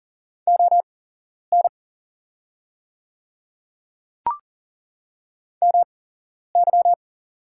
Morse code, heard as a keyed tone, about 40 wpm, spaced as for 11 wpm, 690 Hz. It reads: ON MY
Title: 8x Character Spacing - Top 200 Words - 40wpm